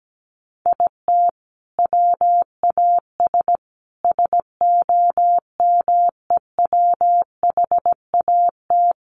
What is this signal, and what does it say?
Morse code, heard as a keyed tone, about 17 wpm, 700 Hz. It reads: IT WAS SOMEWHAT
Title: Sentences from Top 1300 Words 17wpm